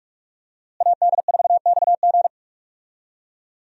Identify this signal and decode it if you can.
Morse code, heard as a keyed tone, about 45 wpm, 700 Hz. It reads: AD4XG
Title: USA Call Signs - 5 Characters - Repeated 2x - 45wpm